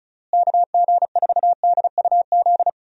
Morse code, heard as a keyed tone, about 34 wpm, 700 Hz. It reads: KG4DUZ
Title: USA Call Signs - 6 Characters - Repeated 3x - 35wpm